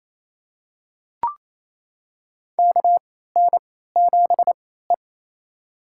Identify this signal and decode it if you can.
Morse code, heard as a keyed tone, about 28 wpm, spaced as for 16 wpm, 700 Hz. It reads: KN7E
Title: USA Call Signs - 4 Characters - 3x Character Spacing - 28wpm